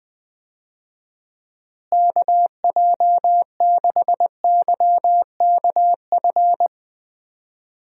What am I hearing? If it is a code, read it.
Morse code, heard as a keyed tone, about 20 wpm, 700 Hz. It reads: KJ6YKF
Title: USA Call Signs - 6 Characters - Repeated 2x - 20wpm